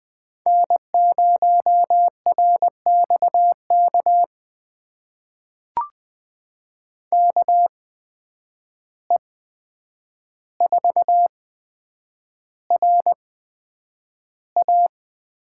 Morse code, heard as a keyed tone, about 20 wpm, spaced as for 9 wpm, 700 Hz. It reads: N0RXK KE4RA
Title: USA Call Signs - 5 Characters - 8x Character Spacing - 20wpm